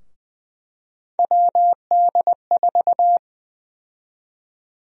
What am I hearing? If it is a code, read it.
Morse code, heard as a keyed tone, about 20 wpm, 700 Hz. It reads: WD4
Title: USA Call Sign Prefixes - Repeated 2x - 20wpm